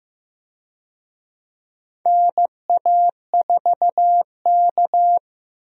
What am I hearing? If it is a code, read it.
Morse code, heard as a keyed tone, about 15 wpm, 700 Hz. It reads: NA4K
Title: USA Call Signs - 4 Characters - Repeated 2x - 15wpm